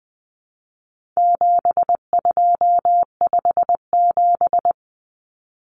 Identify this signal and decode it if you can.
Morse code, heard as a keyed tone, about 20 wpm, 700 Hz. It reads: 7257
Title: Numbers - 4-Digit 20wpm